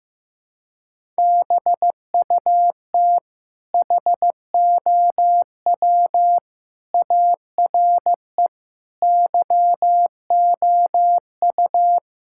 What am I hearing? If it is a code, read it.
Morse code, heard as a keyed tone, about 15 wpm, 700 Hz. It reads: BUT HOW ARE YOU
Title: Sentences from Top 1000 Words - Review Part 6 - 15wpm